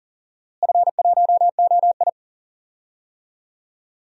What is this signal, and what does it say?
Morse code, heard as a keyed tone, about 40 wpm, 700 Hz. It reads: F1OI
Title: DXCC-100 Call Signs - 4 Characters - Repeated 2x - 40wpm